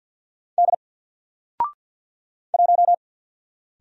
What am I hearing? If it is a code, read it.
Morse code, heard as a keyed tone, about 50 wpm, 700 Hz. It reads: D 1
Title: Single Letter-Number 50wpm